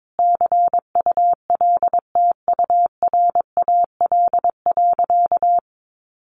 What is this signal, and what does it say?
Morse code, heard as a keyed tone, about 22 wpm, 700 Hz. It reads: CULTURAL.
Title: Sentences from Top 1500 Words 22wpm